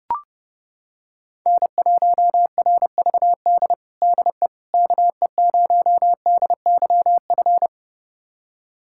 Morse code, heard as a keyed tone, about 30 wpm, 700 Hz. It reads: N1RVD DE KE0DYF
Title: USA Callsign DE Callsign 30wpm